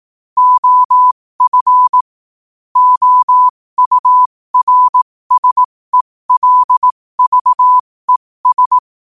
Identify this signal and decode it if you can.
Morse code, heard as a keyed tone, about 18 wpm, spaced as for 15 wpm, 1000 Hz. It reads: OF OURSELVES
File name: qotdfiles / 15wpm